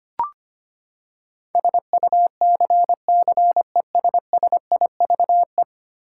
Morse code, heard as a keyed tone, about 25 wpm, 700 Hz. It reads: SUCCESSIVE
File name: Ten Letter Words - 25wpm